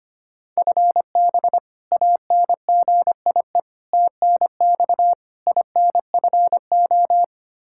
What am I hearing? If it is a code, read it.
Morse code, heard as a keyed tone, about 25 wpm, 700 Hz. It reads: FB ANGIE TNX INFO
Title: K1USN SST QSO Practice 25wpm